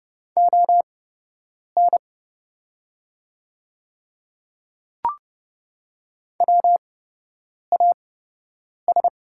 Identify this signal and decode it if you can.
Morse code, heard as a keyed tone, about 30 wpm, spaced as for 8 wpm, 700 Hz. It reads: ON WAS
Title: 8x Character Spacing - Top 100 Words - 30wpm